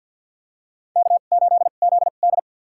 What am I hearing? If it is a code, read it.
Morse code, heard as a keyed tone, about 46 wpm, spaced as for 34 wpm, 700 Hz. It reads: K8ZD
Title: USA Call Signs - 4 Characters - 2x Character Spacing - 50wpm